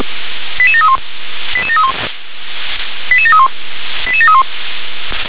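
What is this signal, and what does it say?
Five-tone selective call (ZVEI), heard as quick runs of five tones, tones of about 70 ms. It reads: 88731 88731 88731 88731
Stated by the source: ZVEI five-tone selective call received by SDR.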